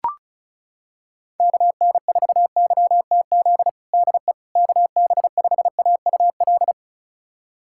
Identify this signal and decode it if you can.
Morse code, heard as a keyed tone, about 34 wpm, 700 Hz. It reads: KN4YTZ DE KB5AUL
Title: USA Callsign DE Callsign - 6 Characters - 35wpm